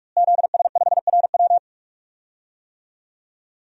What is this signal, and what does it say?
Morse code, heard as a keyed tone, about 42 wpm, 700 Hz. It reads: ZS5RW
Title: DXCC-100 Call Signs - 5 Characters - Repeated 3x - 45wpm